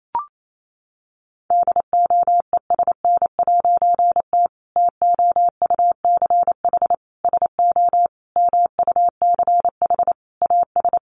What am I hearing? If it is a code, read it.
Morse code, heard as a keyed tone, about 28 wpm, 700 Hz. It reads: DOESN'T TOUCH SO MUCH AS
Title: A Sound Of Thunder 28wpm